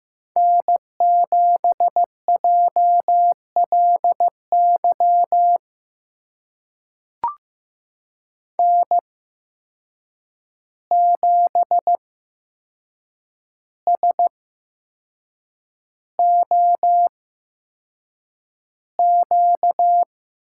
Morse code, heard as a keyed tone, about 15 wpm, spaced as for 6 wpm, 700 Hz. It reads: N7JLY N7SOQ
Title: USA Call Signs - 5 Characters - 8x Character Spacing - 15wpm